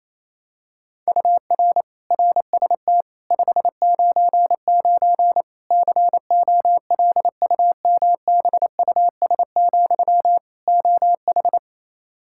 Morse code, heard as a keyed tone, about 27 wpm, 700 Hz. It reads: UR RST 599 COLUMBUS, OH
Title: K1USN SST QSO Practice 28wpm